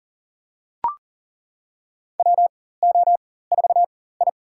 Morse code, heard as a keyed tone, about 40 wpm, spaced as for 19 wpm, 700 Hz. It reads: WO4I
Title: USA Call Signs - 4 Characters - 4x Character Spacing - 40wpm